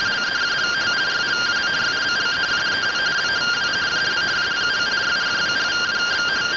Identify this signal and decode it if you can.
Two-tone teleprinter signal, about 170 Hz shift.